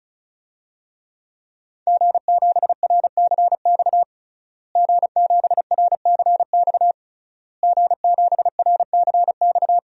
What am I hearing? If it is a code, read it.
Morse code, heard as a keyed tone, about 36 wpm, 700 Hz. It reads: G7RCX G7RCX G7RCX
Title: DXCC-100 Call Signs - 5 Characters - Repeated 3x - 35wpm